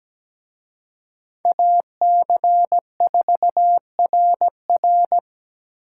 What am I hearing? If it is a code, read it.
Morse code, heard as a keyed tone, about 20 wpm, 700 Hz. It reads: AC4RR
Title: Speed Racing - USA Call Signs 17wpm